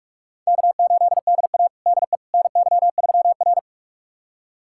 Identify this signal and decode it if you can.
Morse code, heard as a keyed tone, about 44 wpm, 690 Hz. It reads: K8DA DE NY3R
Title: USA Callsign DE Callsign - 4 Characters - 45wpm